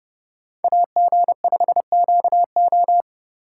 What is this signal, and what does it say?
Morse code, heard as a keyed tone, about 28 wpm, 700 Hz. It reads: AG5QO